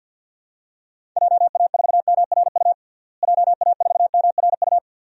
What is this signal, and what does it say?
Morse code, heard as a keyed tone, about 48 wpm, 700 Hz. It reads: JA4MRU JA4MRU
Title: DXCC-100 Call Signs - 6 Characters - Repeated 2x - 50wpm